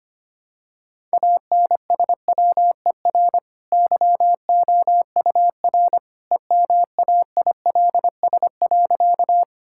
Morse code, heard as a keyed tone, about 25 wpm, 690 Hz. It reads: ANSWER YOUR EMAILS.
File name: Easy Sentences with Hints - 25wpm